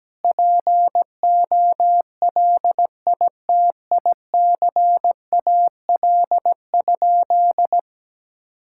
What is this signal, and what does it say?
Morse code, heard as a keyed tone, about 17 wpm, 700 Hz. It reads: POLITICAL?
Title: Sentences from Top 1000 Words - Review Part 5 - 17wpm